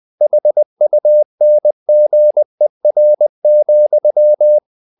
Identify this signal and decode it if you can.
Morse code, heard as a keyed tone, about 20 wpm, 600 Hz. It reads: HUNGER,
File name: the old man and the sea part 2 20wpm